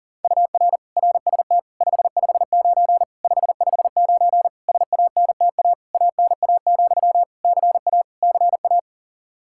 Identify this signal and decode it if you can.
Morse code, heard as a keyed tone, about 40 wpm, 700 Hz. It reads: UR RST 559 559 SANTA ANA, CA CA
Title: Practice QSO - Exchange 1 - Beginning - 40wpm